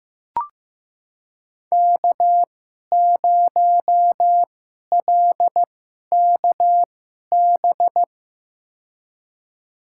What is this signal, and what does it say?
Morse code, heard as a keyed tone, about 15 wpm, spaced as for 11 wpm, 700 Hz. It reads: K0LKB
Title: USA Call Signs - 2x Character Spacing - 15wpm